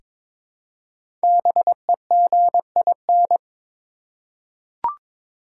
Morse code, heard as a keyed tone, about 23 wpm, 690 Hz. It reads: BEGIN
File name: 5 Letter Words - First 3 Letters Given - 22wpm